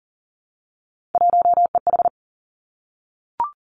Morse code, heard as a keyed tone, about 40 wpm, 700 Hz. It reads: 1EH
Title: Three Challenging Letter-Number Combinations 40wpm